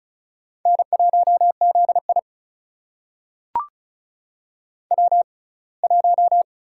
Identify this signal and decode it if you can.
Morse code, heard as a keyed tone, about 36 wpm, spaced as for 24 wpm, 700 Hz. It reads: N1ZI W1
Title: USA Call Signs - 6x Character Spacing - 35wpm